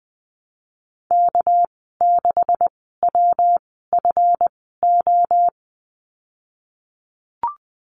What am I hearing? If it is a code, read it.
Morse code, heard as a keyed tone, about 20 wpm, spaced as for 14 wpm, 700 Hz. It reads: K6WFO
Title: 200 Lesson 32g - Letter k - Call Signs 20wpm